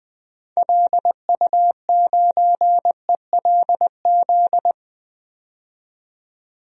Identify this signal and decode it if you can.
Morse code, heard as a keyed tone, about 20 wpm, 700 Hz. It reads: LU9ELZ
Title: DXCC-100 Call Signs - 6 Characters - 20wpm